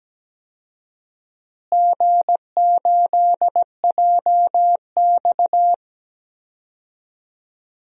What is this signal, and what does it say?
Morse code, heard as a keyed tone, about 17 wpm, 700 Hz. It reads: G8JX
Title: DXCC-100 Call Signs - 4 Characters - Repeated 2x - 17wpm